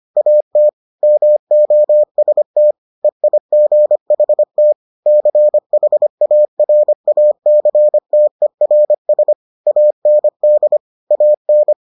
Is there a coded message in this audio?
Yes: Morse code from 0.2 to 11.7 s, heard as a keyed tone, about 25 wpm, 600 Hz. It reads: AT MOST EIGHT CHARACTERS AND AN